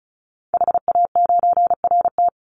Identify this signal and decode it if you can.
Morse code, heard as a keyed tone, about 36 wpm, 700 Hz. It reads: HA9RT
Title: Call Signs - 35wpm